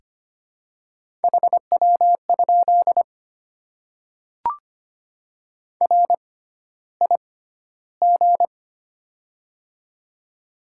Morse code, heard as a keyed tone, about 25 wpm, spaced as for 13 wpm, 700 Hz. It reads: HW? RIG
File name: Top 20 QSO Elements - 6x Character Spacing - 25wpm